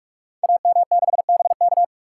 Morse code, heard as a keyed tone, about 44 wpm, 700 Hz. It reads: AM6BK
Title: Code Groups - 5 Letter:Numbers - 45wpm